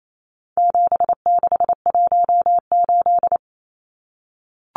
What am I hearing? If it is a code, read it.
Morse code, heard as a keyed tone, about 28 wpm, 700 Hz. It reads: 7618
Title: Numbers - 4-Digit 28wpm